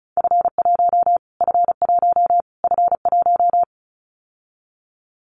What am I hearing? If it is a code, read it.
Morse code, heard as a keyed tone, about 34 wpm, 700 Hz. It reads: F1 F1 F1
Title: Callsign Prefixes for Top 25 DXCC - Repeated 3x - 35wpm